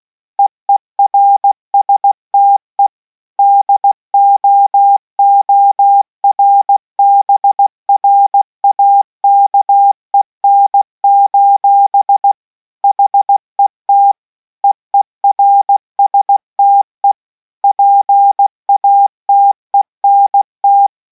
Morse code, heard as a keyed tone, about 16 wpm, 800 Hz. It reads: EERSTE DOORBRAKEN: HET EERSTE PATENT